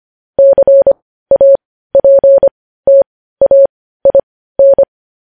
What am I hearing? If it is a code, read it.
Morse code, heard as a keyed tone, about 25 wpm, spaced as for 15 wpm, 550 Hz. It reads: CAPTAIN
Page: news headlines in morse code at 15 wpm